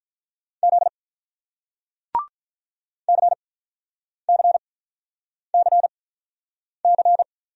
Morse code, heard as a keyed tone, about 47 wpm, spaced as for 9 wpm, 700 Hz. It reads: Z CCCC